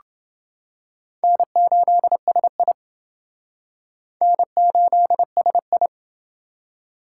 Morse code, heard as a keyed tone, about 28 wpm, 700 Hz. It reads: N8SI N8SI